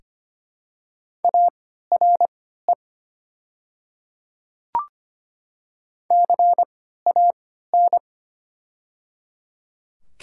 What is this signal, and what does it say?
Morse code, heard as a keyed tone, about 25 wpm, spaced as for 14 wpm, 700 Hz. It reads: ARE CAN
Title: 3x Character Spacing - Top 300 Words - 25wpm